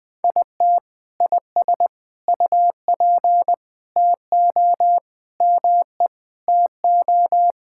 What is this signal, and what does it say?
Morse code, heard as a keyed tone, about 21 wpm, 700 Hz. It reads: IT IS UP TO ME TO
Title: Sentences with 2-Word Phrases - Spoken Beginning - 20wpm